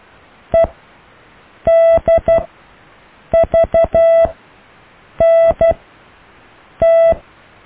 Morse code, heard as a keyed tone, about 12 wpm, spaced as for 6 wpm, 660 Hz. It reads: EDVNT
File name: cw / EKTPV+BFHNW